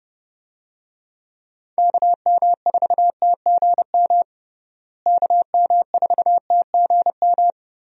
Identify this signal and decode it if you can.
Morse code, heard as a keyed tone, about 30 wpm, 700 Hz. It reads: KM4TGM KM4TGM
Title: USA Call Signs - 6 Characters - Repeated 2x - 30wpm